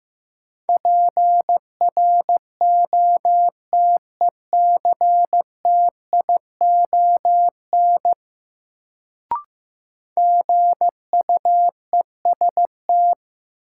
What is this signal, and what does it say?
Morse code, heard as a keyed tone, about 15 wpm, 700 Hz. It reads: PROTECTION GUEST
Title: Sentences from Top 1600 Words - New Word Drill - 15wpm